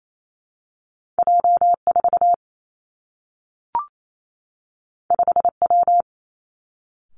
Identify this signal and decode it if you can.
Morse code, heard as a keyed tone, about 28 wpm, 700 Hz. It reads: J 4 5 W
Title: Two Challenging Letter-Number Combinations 28wpm